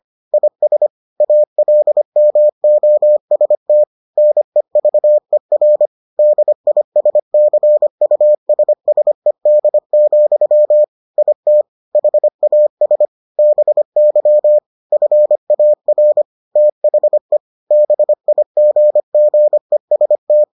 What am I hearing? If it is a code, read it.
Morse code, heard as a keyed tone, about 25 wpm, 600 Hz. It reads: IS ALMOST NEVER DISCUSSED, IT HAS BY FAR THE BIGGEST